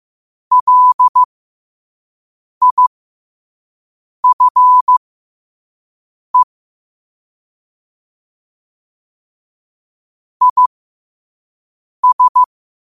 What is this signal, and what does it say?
Morse code, heard as a keyed tone, about 15 wpm, spaced as for 5 wpm, 1000 Hz. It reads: LIFE IS
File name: qotdfiles / 5wpm